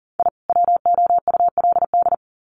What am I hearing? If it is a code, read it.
Morse code, heard as a keyed tone, about 40 wpm, 700 Hz. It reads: I WOULD
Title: Sets of 2 Words 40wpm